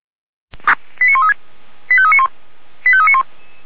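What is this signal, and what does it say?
Five-tone selective call (ZVEI), heard as quick runs of five tones, tones of about 70 ms. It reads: 89136 85391 85391
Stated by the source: Five-tone ZVEI selective call received by SDR.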